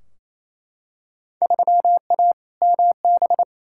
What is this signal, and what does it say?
Morse code, heard as a keyed tone, about 28 wpm, 700 Hz. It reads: 3A MB
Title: Field Day Exchanges Encore - 28wpm